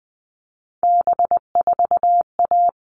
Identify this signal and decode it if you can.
Morse code, heard as a keyed tone, about 20 wpm, 700 Hz. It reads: B4A
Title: All Cut Numbers - 3-Digit 20wpm